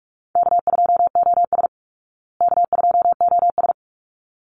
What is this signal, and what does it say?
Morse code, heard as a keyed tone, about 44 wpm, 700 Hz. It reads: K2OS K2OS
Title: Call Signs Repeated 3x - 45wpm